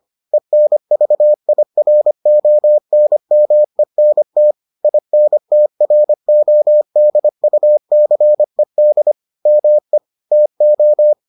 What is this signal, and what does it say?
Morse code, heard as a keyed tone, about 25 wpm, 600 Hz. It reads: ENVIRONMENT INTRODUCED ME TO